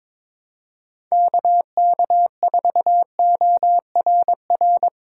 Morse code, heard as a keyed tone, about 22 wpm, 700 Hz. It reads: KK4ORR
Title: USA Call Signs - 6 Characters - 4x Character Spacing - 22wpm